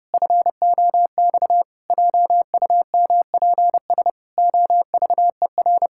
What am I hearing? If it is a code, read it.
Morse code, heard as a keyed tone, about 32 wpm, 700 Hz. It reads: FOX JUMPS OVER
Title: Pangram Challenge v2 - 30wpm